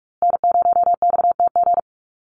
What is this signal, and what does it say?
Morse code, heard as a keyed tone, about 44 wpm, 700 Hz. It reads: N0XTG
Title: USA Call Signs Repeated 3x 45wpm